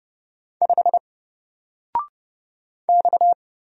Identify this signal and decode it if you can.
Morse code, heard as a keyed tone, about 29 wpm, 700 Hz. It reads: 5 X
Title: Single Character (Letter:Number:Punctuation:Prosign) Alt 30wpm